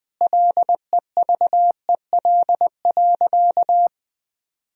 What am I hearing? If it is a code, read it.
Morse code, heard as a keyed tone, about 20 wpm, 700 Hz. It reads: LEVEL.
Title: Sentences from Top 1000 Words - Review Part 4 - 20wpm